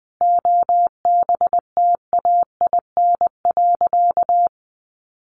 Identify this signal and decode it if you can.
Morse code, heard as a keyed tone, about 20 wpm, 700 Hz. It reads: OBTAIN.
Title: Sentences from Top 1300 Words 20wpm